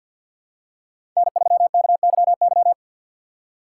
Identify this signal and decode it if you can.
Morse code, heard as a keyed tone, about 52 wpm, 700 Hz. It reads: N3KYY
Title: USA Call Signs - 5 Characters - 2x Character Spacing - 50wpm